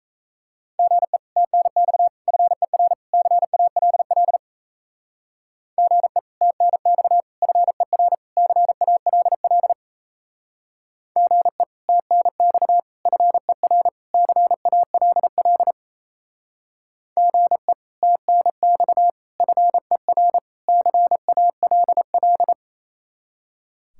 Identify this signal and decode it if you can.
Morse code, heard as a keyed tone, about 34 wpm, 700 Hz. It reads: GE TNX FER CALL GE TNX FER CALL GE TNX FER CALL GE TNX FER CALL
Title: Speed-Racing Common QSO Phrases - 28wpm